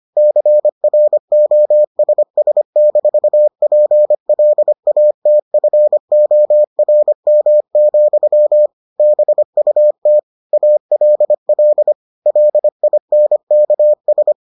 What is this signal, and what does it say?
Morse code, heard as a keyed tone, about 25 wpm, 600 Hz. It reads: CROSS-PLATFORM, BUT ALL LINKS